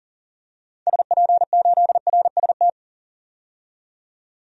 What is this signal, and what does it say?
Morse code, heard as a keyed tone, about 40 wpm, 700 Hz. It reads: SP8RST
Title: DXCC-100 Call Signs - 6 Characters - 40wpm